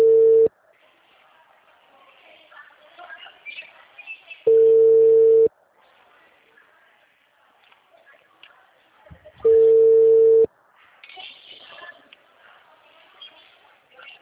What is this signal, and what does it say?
Ringback tone (China, 450 Hz), with speech.